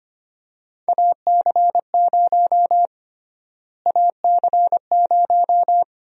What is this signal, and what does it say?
Morse code, heard as a keyed tone, about 25 wpm, 700 Hz. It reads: AC0 AC0
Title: USA Call Sign Prefixes - Repeated 2x - 25wpm